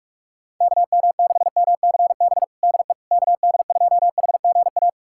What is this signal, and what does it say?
Morse code, heard as a keyed tone, about 42 wpm, 700 Hz. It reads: KM6MCB DE KD2HGA
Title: USA Callsign DE Callsign 45wpm